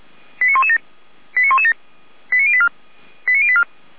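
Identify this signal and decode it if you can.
Five-tone selective call (ZVEI), heard as quick runs of five tones, tones of about 70 ms. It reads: 89117 89117 89084 89084
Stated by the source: ZVEI five-tone selective call 89503 received on SDR.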